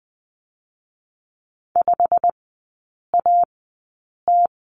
Morse code, heard as a keyed tone, about 19 wpm, 700 Hz. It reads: 5 A T
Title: 121 Lesson 20e - Number 3 - Three Characters 20wpm